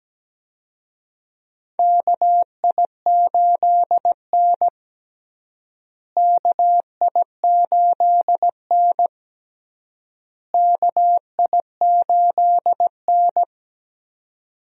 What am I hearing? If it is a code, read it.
Morse code, heard as a keyed tone, about 17 wpm, 700 Hz. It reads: KI8N KI8N KI8N
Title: USA Call Signs - 4 Characters - Repeated 3x - 17wpm